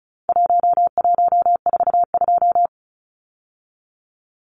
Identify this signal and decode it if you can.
Morse code, heard as a keyed tone, about 34 wpm, 700 Hz. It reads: 1142
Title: Numbers - Contest Style - 4-Digit 35wpm